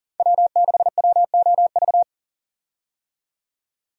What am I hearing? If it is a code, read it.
Morse code, heard as a keyed tone, about 40 wpm, 700 Hz. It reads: W6WOV
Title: USA Call Signs - 5 Characters - 40wpm